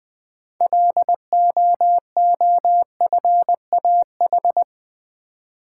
Morse code, heard as a keyed tone, about 20 wpm, 700 Hz. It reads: LOOFAH
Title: Difficult Words with Hints - 20wpm